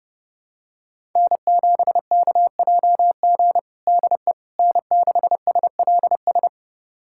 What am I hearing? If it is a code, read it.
Morse code, heard as a keyed tone, about 32 wpm, 700 Hz. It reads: N7KJG DE N6SLS
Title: USA Callsign DE Callsign - 5 Characters - 30wpm